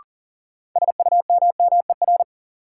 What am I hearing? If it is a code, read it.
Morse code, heard as a keyed tone, about 39 wpm, 700 Hz. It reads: SUMMER
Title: Sentences from Top 1100 Words - New Word Drill Alt - 40wpm